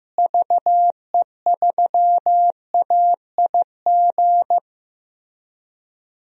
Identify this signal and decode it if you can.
Morse code, heard as a keyed tone, about 15 wpm, 700 Hz. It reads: VE3AIG
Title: DXCC-100 Call Signs - 6 Characters - Repeated 3x - 15wpm